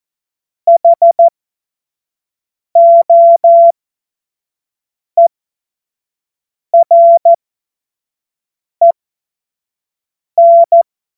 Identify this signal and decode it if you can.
Morse code, heard as a keyed tone, about 14 wpm, spaced as for 5 wpm, 680 Hz. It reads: HOEREN
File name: Morsekurs / Witze / Witz 04. Januar 2022, 25 F70